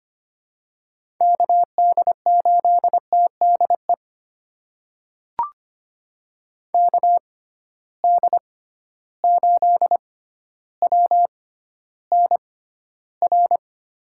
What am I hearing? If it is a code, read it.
Morse code, heard as a keyed tone, about 25 wpm, spaced as for 13 wpm, 700 Hz. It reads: KD8TDE KD8WNR
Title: USA Call Signs - 6 Characters - 6x Character Spacing - 25wpm